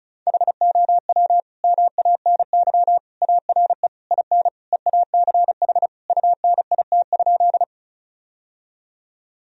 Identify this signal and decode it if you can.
Morse code, heard as a keyed tone, about 36 wpm, 700 Hz. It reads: HOW MANY ARE IN EACH UNIT?
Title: Sentences from Top 1000 Words - Review Part 2 - 35wpm